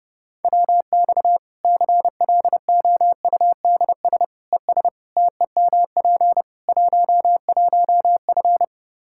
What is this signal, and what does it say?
Morse code, heard as a keyed tone, about 32 wpm, 700 Hz. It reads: WX CLOUDS ES TEMP 11F
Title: Practice QSO - Exchange 2 - Info - 30wpm